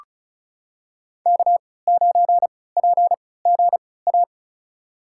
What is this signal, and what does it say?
Morse code, heard as a keyed tone, about 34 wpm, spaced as for 20 wpm, 700 Hz. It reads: K9PGA
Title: USA Call Signs - 5 Characters - 3x Character Spacing - 35wpm